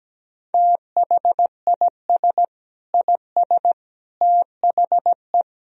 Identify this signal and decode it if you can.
Morse code, heard as a keyed tone, about 17 wpm, 700 Hz. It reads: THIS IS THE
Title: Speed-Racing - Sets of 5 Words 17wpm